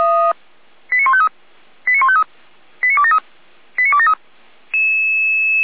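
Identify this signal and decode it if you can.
Five-tone selective call (ZVEI), heard as quick runs of five tones, tones of about 70 ms. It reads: 89153 89153 89273 89273
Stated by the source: ZVEI five-tone selective call 89270, received by SDR.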